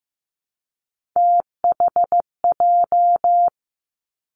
Morse code, heard as a keyed tone, about 15 wpm, 700 Hz. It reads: THJ
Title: Three Challenging Letter-Number Combinations 15wpm